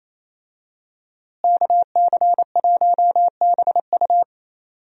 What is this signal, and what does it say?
Morse code, heard as a keyed tone, about 27 wpm, 700 Hz. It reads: KC1BU
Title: USA Call Signs - 5 Characters - Repeated 3x - 28wpm